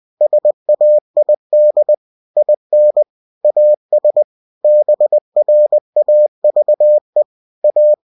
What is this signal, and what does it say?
Morse code, heard as a keyed tone, about 20 wpm, 600 Hz. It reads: SAID IN AS BRAVE A
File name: cw-books / winnie-the-pooh / winnie-20wpm-page-0071